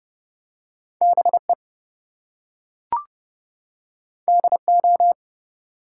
Morse code, heard as a keyed tone, about 30 wpm, 700 Hz. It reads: BE DO
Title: Two Letter Words - 30wpm